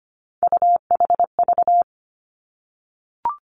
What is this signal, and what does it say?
Morse code, heard as a keyed tone, about 25 wpm, 700 Hz. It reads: UHV
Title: Three Challenging Letter-Number Combinations 25wpm